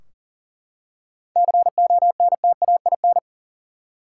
Morse code, heard as a keyed tone, about 40 wpm, 700 Hz. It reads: CONTAIN
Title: Seven Letter Words - 40wpm